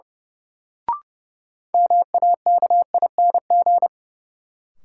Morse code, heard as a keyed tone, about 32 wpm, 700 Hz. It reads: MAKING
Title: Sentences from Top 1000 Words - Review Part 6 - New Word Drill - 30wpm